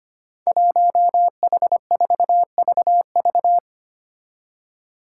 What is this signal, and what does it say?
Morse code, heard as a keyed tone, about 25 wpm, 700 Hz. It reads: 1H4VV
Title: Five Challenging Letter-Number Combinations 25wpm